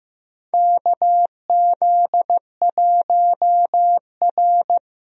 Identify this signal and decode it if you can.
Morse code, heard as a keyed tone, about 15 wpm, 700 Hz. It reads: KZ1R
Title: USA Call Signs - 4 Characters - 15wpm